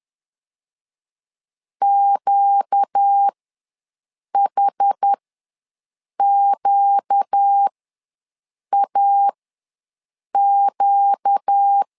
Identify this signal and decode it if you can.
Morse code, heard as a keyed tone, about 10 wpm, spaced as for 6 wpm, 800 Hz. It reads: QHQAQ
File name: LEZIONI / 4 Q ws 10wpm 2306